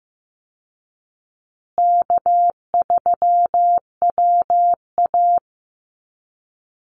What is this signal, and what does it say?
Morse code, heard as a keyed tone, about 15 wpm, 700 Hz. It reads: K3WA
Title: Call Signs Repeated 2x - 15wpm